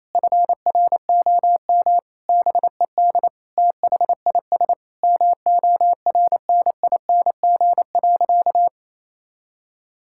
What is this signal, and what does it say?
Morse code, heard as a keyed tone, about 27 wpm, 700 Hz. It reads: FROM BED THIS MORNING.
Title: Sentences from Top 1000 Words - Review Part 6 - 28wpm